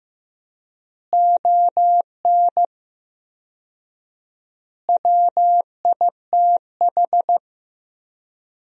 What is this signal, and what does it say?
Morse code, heard as a keyed tone, about 15 wpm, 700 Hz. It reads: ON WITH
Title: 4x Word-Spacing - Sentences from Top 300 Words - 15wpm